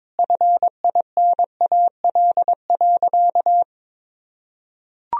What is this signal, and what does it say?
Morse code, heard as a keyed tone, about 22 wpm, 700 Hz. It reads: FINAL.
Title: Sentences from Top 1000 Words - Review Part 2 - 22wpm